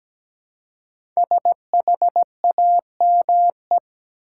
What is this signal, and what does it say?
Morse code, heard as a keyed tone, about 17 wpm, 700 Hz. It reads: SHAME
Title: Five Letter Words - 17wpm